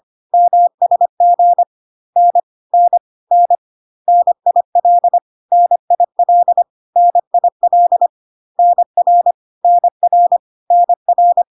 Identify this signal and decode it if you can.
Morse code, heard as a keyed tone, about 25 wpm, 700 Hz. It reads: MSG N N N NIL NIL NIL NR NR NR
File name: cw / CW common abbreviations alphabetical repeated 3 times 25 25wpm ik1hge